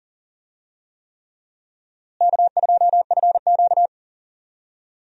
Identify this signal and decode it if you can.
Morse code, heard as a keyed tone, about 40 wpm, 700 Hz. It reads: K2FQ
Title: USA Call Signs - 4 Characters - Repeated 3x - 40wpm